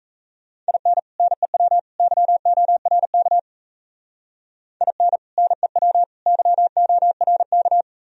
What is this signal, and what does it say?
Morse code, heard as a keyed tone, about 35 wpm, 700 Hz. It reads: IN NEW YORK IN NEW YORK
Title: Speed-Racing - Sets of 3 Words 28wpm